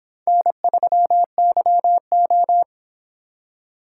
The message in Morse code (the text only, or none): N3YO